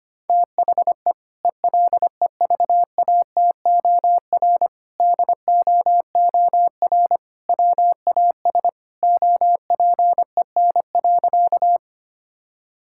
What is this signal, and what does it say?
Morse code, heard as a keyed tone, about 25 wpm, 700 Hz. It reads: THE ELEVATOR DOOR WAS OPEN.
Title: Sentences from Top 1600 Words 25wpm